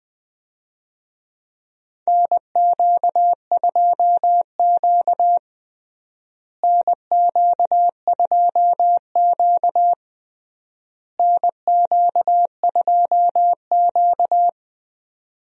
Morse code, heard as a keyed tone, about 20 wpm, 700 Hz. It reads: NQ2Q NQ2Q NQ2Q
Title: USA Call Signs - 4 Characters - Repeated 3x - 20wpm